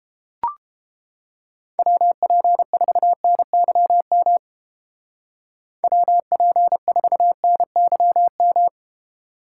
Morse code, heard as a keyed tone, about 29 wpm, 700 Hz. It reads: WP4NYM WP4NYM